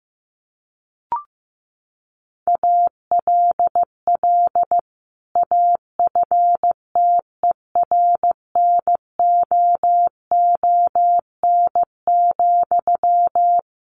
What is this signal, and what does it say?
Morse code, heard as a keyed tone, about 15 wpm, 700 Hz. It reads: ALL AFTERNOON,
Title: Sentences from Top 1500 Words 15wpm